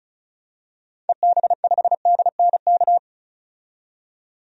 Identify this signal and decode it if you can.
Morse code, heard as a keyed tone, about 36 wpm, 700 Hz. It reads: EB5DNK
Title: DXCC-100 Call Signs - 6 Characters - 35wpm